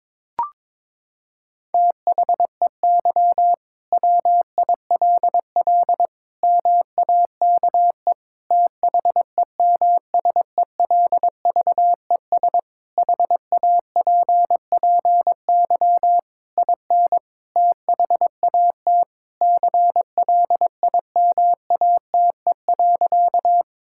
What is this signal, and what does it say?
Morse code, heard as a keyed tone, about 21 wpm, 700 Hz. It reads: THEY WILL MAKE THEMSELVES HAPPY IN THAT CLIMATE.
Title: Sentences from Top 1600 Words 22wpm